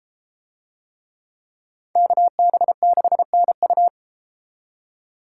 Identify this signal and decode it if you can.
Morse code, heard as a keyed tone, about 33 wpm, 700 Hz. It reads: KB6NU